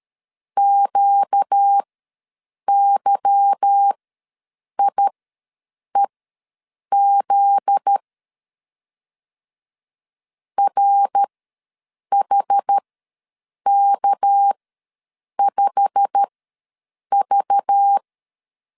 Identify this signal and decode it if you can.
Morse code, heard as a keyed tone, about 13 wpm, spaced as for 7 wpm, 800 Hz. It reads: QYIEZ RHK5V
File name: LEZIONI / 8 YJ45 / 8 YJ45 ws 12wpm 3146